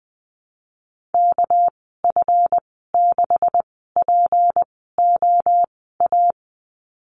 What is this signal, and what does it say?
Morse code, heard as a keyed tone, about 20 wpm, spaced as for 14 wpm, 700 Hz. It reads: KF6POA
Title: 238 Lesson 38g - Letter q - Call Signs 20wpm